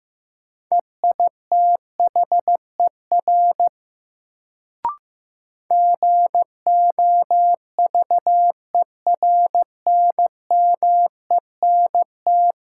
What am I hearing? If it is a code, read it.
Morse code, heard as a keyed tone, about 15 wpm, 700 Hz. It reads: EITHER GOVERNMENT
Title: Sentences from Top 600 Words - New Word Drill - 15wpm